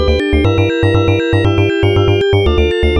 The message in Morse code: none